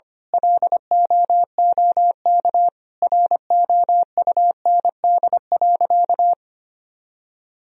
Morse code, heard as a keyed tone, about 25 wpm, 700 Hz. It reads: LOOK ROUND.
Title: Sentences from Top 1000 Words - Review Part 1 - 25wpm